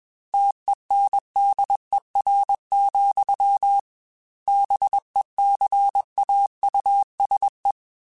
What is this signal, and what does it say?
Morse code, heard as a keyed tone, about 21 wpm, 800 Hz. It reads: TENDER, BECAUSE
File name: Chuckles / Chuckles-mp3s / SoftWords @20